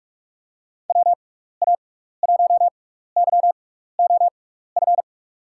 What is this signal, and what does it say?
Morse code, heard as a keyed tone, about 44 wpm, spaced as for 15 wpm, 700 Hz. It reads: WA1YOF
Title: USA Call Signs - 6 Characters - 6x Character Spacing - 45wpm